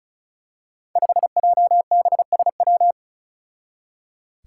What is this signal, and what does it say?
Morse code, heard as a keyed tone, about 36 wpm, 700 Hz. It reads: 5JBSW
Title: Five Challenging Letter-Number Combinations 35wpm